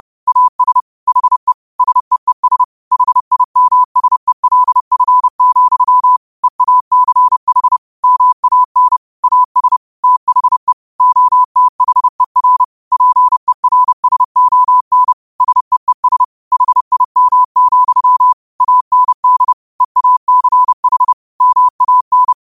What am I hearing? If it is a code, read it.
Morse code, heard as a keyed tone, about 30 wpm, 1000 Hz. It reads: AS HE SEES HIMSELF, EACH MAN AS THE OTHER PERSON SEES HIM, AND EACH MAN
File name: qotdfiles / 30wpm